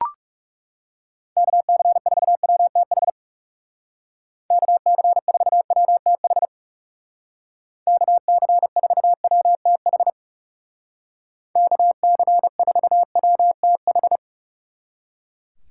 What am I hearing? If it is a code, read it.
Morse code, heard as a keyed tone, about 38 wpm, 700 Hz. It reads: KC4WTH KC4WTH KC4WTH KC4WTH